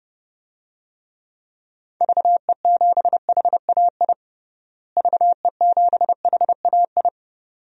Morse code, heard as a keyed tone, about 32 wpm, 700 Hz. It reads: VE7HAI VE7HAI
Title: DXCC-100 Call Signs - 6 Characters - Repeated 2x - 30wpm